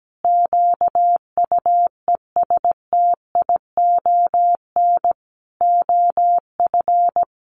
Morse code, heard as a keyed tone, about 17 wpm, 700 Hz. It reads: QUESTION OF
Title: Sentences from Top 1400 Words 17wpm